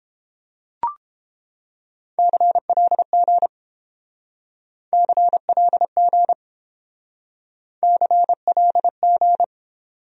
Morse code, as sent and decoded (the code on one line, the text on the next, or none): -.-. .-.. --. / -.-. .-.. --. / -.-. .-.. --.
CLG CLG CLG